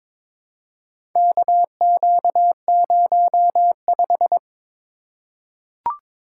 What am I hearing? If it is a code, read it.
Morse code, heard as a keyed tone, about 22 wpm, 700 Hz. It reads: KQ05
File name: practice / 22wpm / Four Letter-Number Combinations - 22wpm